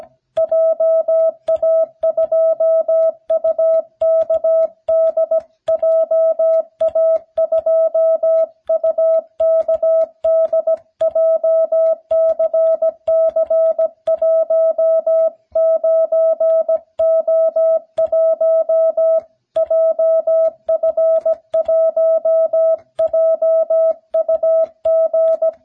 Morse code, heard as a keyed tone, about 16 wpm, 650 Hz. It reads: JA2UKD JA2UKD JCC19O1 JF1JUG